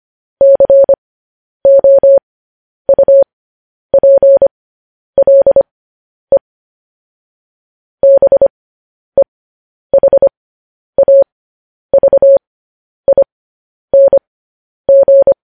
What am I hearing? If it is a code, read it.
Morse code, heard as a keyed tone, about 25 wpm, spaced as for 10 wpm, 550 Hz. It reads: COUPLE BEHAVING
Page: news headlines in morse code at 10 wpm